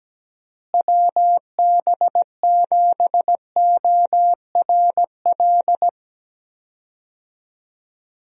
Morse code, heard as a keyed tone, about 17 wpm, 700 Hz. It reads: WB7ORL
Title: USA Call Signs - 6 Characters - Repeated 3x - 17wpm